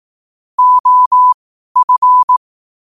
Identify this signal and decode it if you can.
Morse code, heard as a keyed tone, about 18 wpm, spaced as for 12 wpm, 1000 Hz. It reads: OF
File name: qotdfiles / 12wpm